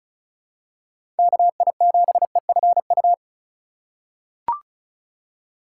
Morse code, heard as a keyed tone, about 36 wpm, 700 Hz. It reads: KI7EFU
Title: USA Call Signs - 3x Character Spacing - 35wpm